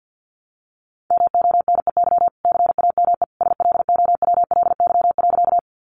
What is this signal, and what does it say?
Morse code, heard as a keyed tone, about 50 wpm, 700 Hz. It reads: MONEY CAME SLOWLY.
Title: Sentences from Top 1100 Words 50wpm